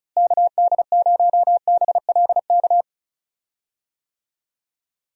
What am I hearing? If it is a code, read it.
Morse code, heard as a keyed tone, about 36 wpm, 700 Hz. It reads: KD0BLK
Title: USA Call Signs - 6 Characters - Repeated 3x - 35wpm